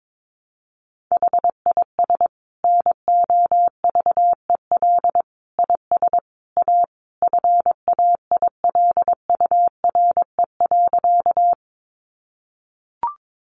Morse code, heard as a keyed tone, about 22 wpm, 700 Hz. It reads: HIS NOVEL IS A FAILURE.
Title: Sentences from Top 1300 Words 22wpm